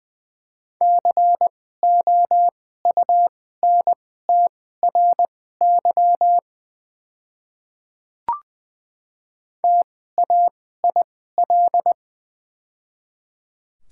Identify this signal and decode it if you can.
Morse code, heard as a keyed tone, about 20 wpm, spaced as for 14 wpm, 700 Hz. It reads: COUNTRY TAIL
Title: 2x Character Spacing - Top 500 Words - 20wpm